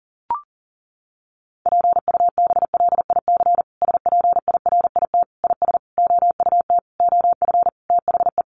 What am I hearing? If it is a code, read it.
Morse code, heard as a keyed tone, about 40 wpm, 700 Hz. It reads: PUBLIC SPIRIT IS OUT OF THE
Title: Sentences from Top 1400 Words 40wpm